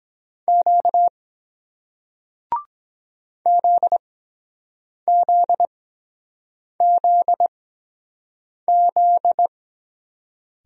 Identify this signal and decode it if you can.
Morse code, heard as a keyed tone, about 23 wpm, spaced as for 7 wpm, 700 Hz. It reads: Q ZZZZ